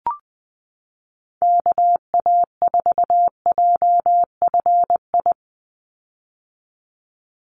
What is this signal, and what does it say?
Morse code, heard as a keyed tone, about 20 wpm, 700 Hz. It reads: KA4JFI
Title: USA Call Signs 20wpm